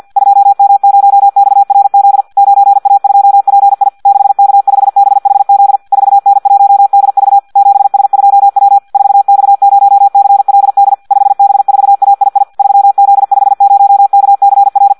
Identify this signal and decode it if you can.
Morse code, heard as a keyed tone, about 50 wpm, 800 Hz. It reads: 2M0XRZ 9A2PI 6K5BHZ 4N1DV 7S2W 4X0CLD 5B4AII 3Z50KCR
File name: Audio practica CW / qrq-tx-50 2